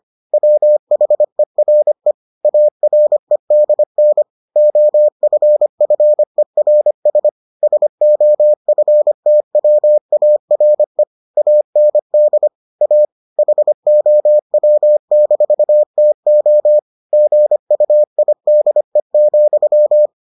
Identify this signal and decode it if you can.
Morse code, heard as a keyed tone, about 25 wpm, 600 Hz. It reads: WHERE AREDN OFFERS SOFTWARE AND A HOW-TO GUIDE,